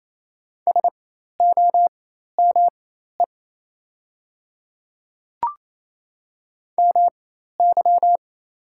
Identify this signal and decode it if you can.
Morse code, heard as a keyed tone, about 28 wpm, spaced as for 13 wpm, 690 Hz. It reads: SOME MY
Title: 4x Character Spacing - Top 100 Words - 28wpm